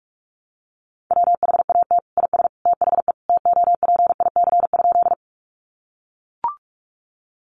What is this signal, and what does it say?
Morse code, heard as a keyed tone, about 44 wpm, 700 Hz. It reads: WHAT IS THE TOPIC?
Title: Sentences from Top 1200 Words 45wpm